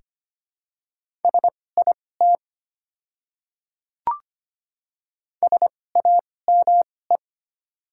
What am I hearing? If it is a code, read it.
Morse code, heard as a keyed tone, about 25 wpm, spaced as for 18 wpm, 700 Hz. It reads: SIT SAME
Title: 2x Character Spacing - Top 500 Words - 25wpm